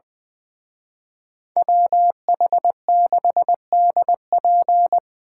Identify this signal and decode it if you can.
Morse code, heard as a keyed tone, about 24 wpm, 700 Hz. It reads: WH6DP
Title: Speed Racing - USA 2x2 Call Signs 20wpm